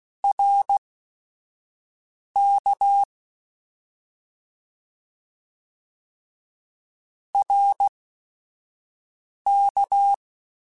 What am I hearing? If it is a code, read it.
Morse code, heard as a keyed tone, about 16 wpm, spaced as for 5 wpm, 800 Hz. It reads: RK RK
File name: Short Runs / Short Runs-mp3s / ShortContrasts @5